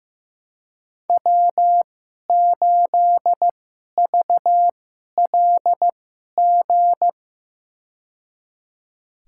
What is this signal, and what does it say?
Morse code, heard as a keyed tone, about 15 wpm, spaced as for 11 wpm, 700 Hz. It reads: W8VLG
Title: USA Call Signs - 5 Characters - 2x Character Spacing - 15wpm